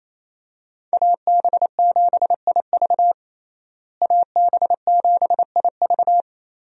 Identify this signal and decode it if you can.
Morse code, heard as a keyed tone, about 28 wpm, 700 Hz. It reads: AB7IV AB7IV
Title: USA Call Signs - 5 Characters - Repeated 3x - 28wpm